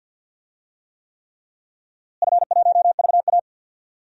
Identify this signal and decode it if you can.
Morse code, heard as a keyed tone, about 46 wpm, 700 Hz. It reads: F1VA
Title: DXCC-100 Call Signs - 4 Characters - Repeated 3x - 50wpm